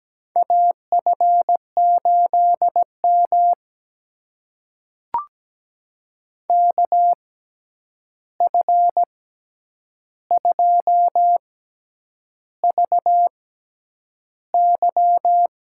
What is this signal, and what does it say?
Morse code, heard as a keyed tone, about 17 wpm, spaced as for 8 wpm, 700 Hz. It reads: AF8M KF2VY
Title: USA Call Signs - 6x Character Spacing - 17wpm